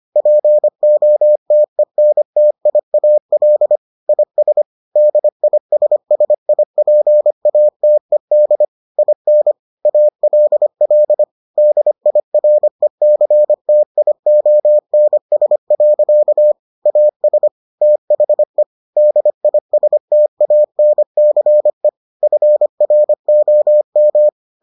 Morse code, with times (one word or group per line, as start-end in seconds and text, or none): 0.2-3.8 POTENTIAL
4.1-4.6 IS
5.0-8.6 DISSIPATED
9.0-9.5 IN
9.9-11.2 ALL
11.6-16.5 DIRECTIONS.
16.9-17.5 AS
17.8-18.6 THE
19.0-21.9 DISTANCE
22.2-24.3 FROM